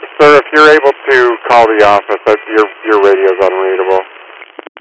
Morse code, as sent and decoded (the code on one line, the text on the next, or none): none